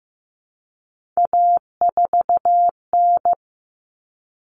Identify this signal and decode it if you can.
Morse code, heard as a keyed tone, about 15 wpm, 700 Hz. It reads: A4N
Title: Common Cut Numbers - 3-Digit 15wpm